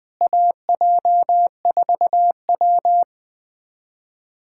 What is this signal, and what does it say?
Morse code, heard as a keyed tone, about 20 wpm, 700 Hz. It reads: AJ4W
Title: USA Callsign DE Callsign - 4 Characters - 20wpm